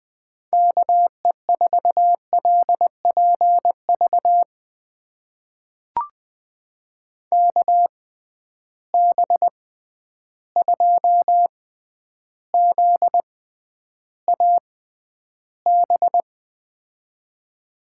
Morse code, heard as a keyed tone, about 20 wpm, spaced as for 10 wpm, 700 Hz. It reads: KE4LPV KB2ZAB
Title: USA Call Signs - 6 Characters - 6x Character Spacing - 20wpm